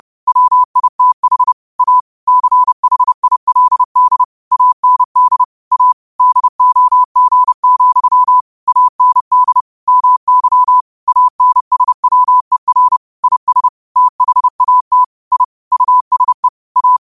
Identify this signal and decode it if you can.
Morse code, heard as a keyed tone, about 30 wpm, 1000 Hz. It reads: WITH A CHILD AND A DOG, AND MY ANSWER IS THAT I USE A
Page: Quotes for Mon, 28 Apr 2025 in Morse Code at 30 words per minute.